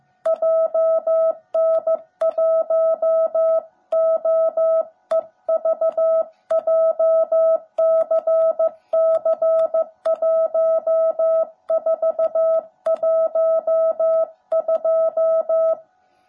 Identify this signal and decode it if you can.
Morse code, heard as a keyed tone, about 15 wpm, 650 Hz. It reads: JN1OEV JCC1412
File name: jh8cbh / 2021cw / cw17b